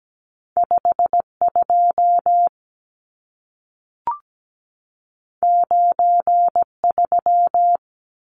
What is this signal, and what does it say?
Morse code, heard as a keyed tone, about 17 wpm, 700 Hz. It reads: 52 93
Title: Numbers - 2-Digit 17wpm